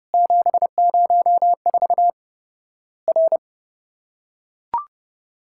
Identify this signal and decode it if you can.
Morse code, heard as a keyed tone, about 30 wpm, 700 Hz. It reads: 704 R
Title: Worked All Europe DX Contest - QTC Exchange - 30wpm